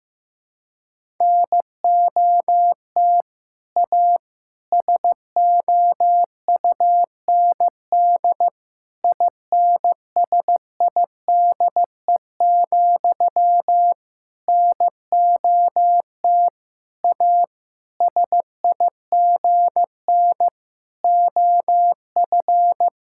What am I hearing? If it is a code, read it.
Morse code, heard as a keyed tone, about 15 wpm, 700 Hz. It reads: NOT A SOUND INSIDE, NOT A SIGN OF
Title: Sentences from Top 1000 Words - Review Part 6 - 15wpm